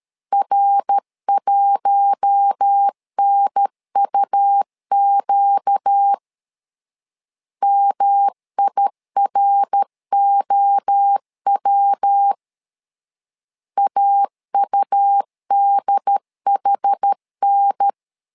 Morse code, heard as a keyed tone, about 13 wpm, 800 Hz. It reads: R1NUQ MIROW AUDHN
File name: LEZIONI / 4 Q1 12wpm 2028